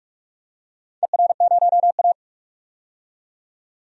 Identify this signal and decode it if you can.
Morse code, heard as a keyed tone, about 42 wpm, 700 Hz. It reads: ER0A
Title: DXCC-100 Call Signs - 4 Characters - 45wpm